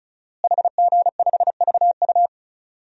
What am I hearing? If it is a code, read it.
Morse code, heard as a keyed tone, about 36 wpm, 690 Hz. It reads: HG5VU
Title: DXCC-100 Call Signs - 5 Characters - Repeated 3x - 35wpm